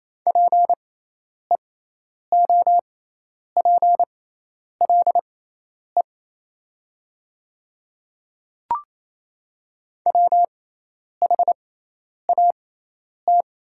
Morse code, heard as a keyed tone, about 27 wpm, spaced as for 10 wpm, 690 Hz. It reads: PEOPLE WHAT
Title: 6x Character Spacing - Top 100 Words - 28wpm